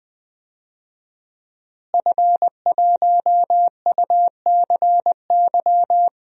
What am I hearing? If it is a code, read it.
Morse code, heard as a keyed tone, about 20 wpm, 700 Hz. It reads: F1UCY
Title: DXCC-100 Call Signs - 5 Characters - Repeated 3x - 20wpm